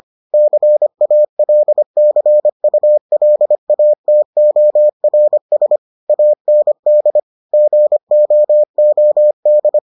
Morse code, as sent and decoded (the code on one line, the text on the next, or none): -.-. .- .-.. -.-. ..- .-.. .- - --- .-. ... / .- -. -.. / --. --- --- -..
CALCULATORS AND GOOD